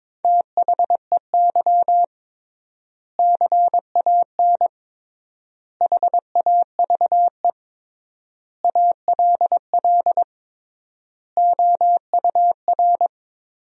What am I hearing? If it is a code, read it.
Morse code, heard as a keyed tone, about 22 wpm, 700 Hz. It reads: THEY CAN HAVE ALL OUR
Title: 3x Word-Spacing - Sentences from Top 400 Words - 22wpm